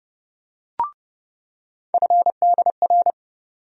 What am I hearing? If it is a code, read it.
Morse code, heard as a keyed tone, about 30 wpm, 700 Hz. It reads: FDR
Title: USA Call Sign Postfixes - 3 Letters 30wpm